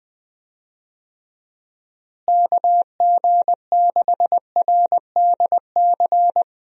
Morse code, heard as a keyed tone, about 20 wpm, 700 Hz. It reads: KG6RDC
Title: USA Call Signs - 6 Characters - Repeated 2x - 20wpm